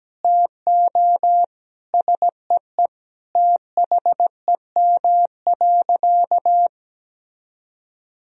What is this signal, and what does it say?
Morse code, heard as a keyed tone, about 17 wpm, 700 Hz. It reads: TO SEE THEM.
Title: Sentences from Top 1000 Words - Review Part 1 - 17wpm